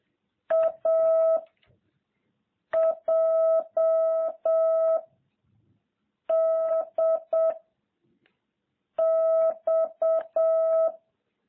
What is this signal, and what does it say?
Morse code, heard as a keyed tone, about 7 wpm, 660 Hz. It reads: A J D X